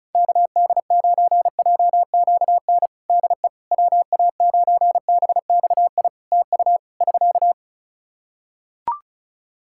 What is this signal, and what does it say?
Morse code, heard as a keyed tone, about 34 wpm, 700 Hz. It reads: KD9JQN DE WA9BXI TU <SK>
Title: Practice QSO - Exchange 4 - Ending - 35wpm